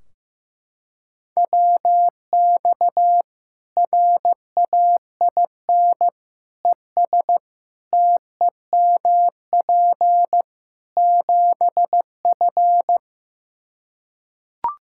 Morse code, heard as a keyed tone, about 15 wpm, 700 Hz. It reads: WX RAIN ES TEMP 7F
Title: Practice QSO - Exchange 2 - Info - 15wpm